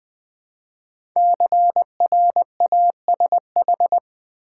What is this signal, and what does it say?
Morse code, heard as a keyed tone, about 20 wpm, 700 Hz. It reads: CRASH
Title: 5 Letter Words - First 3 Letters Given - 20wpm